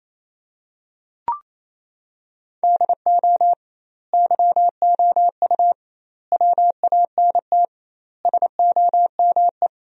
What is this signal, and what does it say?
Morse code, heard as a keyed tone, about 28 wpm, 700 Hz. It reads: DO YOU WANT SOME
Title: 2x Word-Spacing - Sentences from Top 300 Words - 28wpm